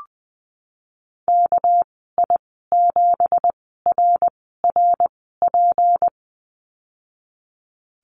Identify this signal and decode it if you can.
Morse code, heard as a keyed tone, about 20 wpm, 700 Hz. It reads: K I 7 R R P
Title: 238 Lesson 38g - Letter q - Call Signs 20wpm